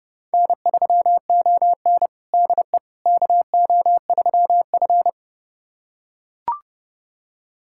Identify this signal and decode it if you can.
Morse code, heard as a keyed tone, about 30 wpm, 700 Hz. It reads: N3ON DE KO3F
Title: USA Callsign DE Callsign - 4 Characters - 30wpm